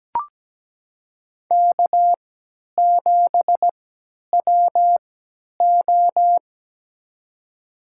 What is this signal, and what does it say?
Morse code, heard as a keyed tone, about 17 wpm, spaced as for 10 wpm, 700 Hz. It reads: K7WO
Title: USA Call Signs - 3x Character Spacing - 17wpm